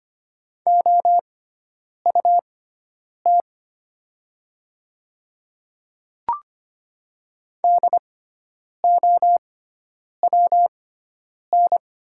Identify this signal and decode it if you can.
Morse code, heard as a keyed tone, about 25 wpm, spaced as for 9 wpm, 700 Hz. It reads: OUT DOWN
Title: 6x Character Spacing - Top 100 Words - 25wpm